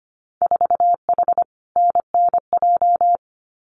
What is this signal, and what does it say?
Morse code, heard as a keyed tone, about 25 wpm, 700 Hz. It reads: 4H NNJ
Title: Winter Field Day Exchanges Encore 25wpm